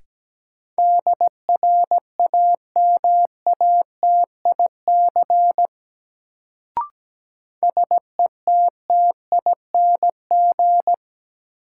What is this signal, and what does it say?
Morse code, heard as a keyed tone, about 17 wpm, 700 Hz. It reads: DRAMATIC SETTING
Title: Sentences from Top 1300 Words - New Word Drill - 17wpm